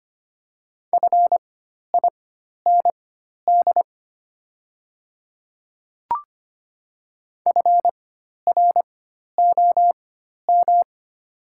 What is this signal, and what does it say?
Morse code, heard as a keyed tone, about 25 wpm, spaced as for 12 wpm, 700 Hz. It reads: FIND FROM
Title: 4x Character Spacing - Top 100 Words - 25wpm